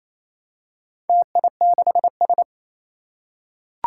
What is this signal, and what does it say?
Morse code, heard as a keyed tone, about 29 wpm, 700 Hz. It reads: TI6S
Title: Four Challenging Letter-Number Combinations 28wpm